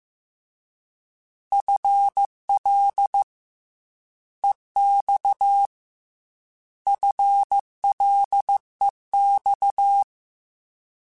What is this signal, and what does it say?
Morse code, heard as a keyed tone, about 15 wpm, 800 Hz. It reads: FL EX FLEX
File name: FL / FL mp3s / Fast FL @13